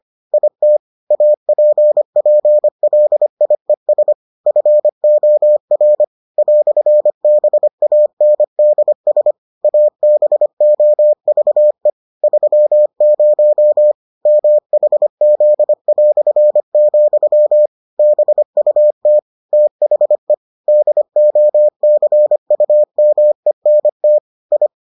Morse code, heard as a keyed tone, about 25 wpm, 600 Hz. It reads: IT APPLIES FOR "BANDS ABOVE 30 MHZ", BUT THE DOCUMENT I